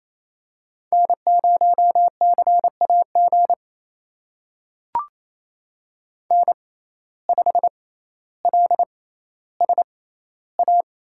Morse code, heard as a keyed tone, about 28 wpm, spaced as for 14 wpm, 700 Hz. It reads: N0CAG N5LSA
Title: USA Call Signs - 5 Characters - 6x Character Spacing - 28wpm